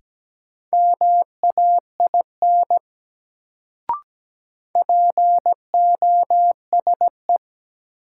Morse code, heard as a keyed tone, about 17 wpm, 700 Hz. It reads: MAIN POSE
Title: Sentences from Top 1000 Words - Review Part 2 - New Word Drill - 17wpm